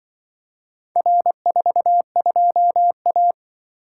Morse code, heard as a keyed tone, about 24 wpm, 700 Hz. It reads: R42A
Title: Speed-Racing - Four Letter-Number Combinations - 20wpm